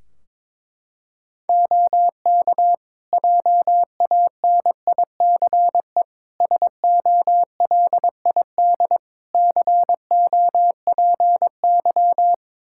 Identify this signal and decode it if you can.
Morse code, heard as a keyed tone, about 22 wpm, 700 Hz. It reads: OK JANICE SOLID COPY
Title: Practice QSO - Exchange 3 - More Info - 22wpm